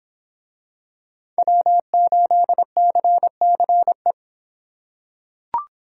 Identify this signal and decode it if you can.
Morse code, heard as a keyed tone, about 25 wpm, 700 Hz. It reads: W8CCE